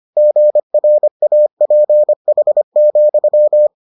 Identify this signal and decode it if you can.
Morse code, heard as a keyed tone, about 25 wpm, 600 Hz. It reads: GRAPH,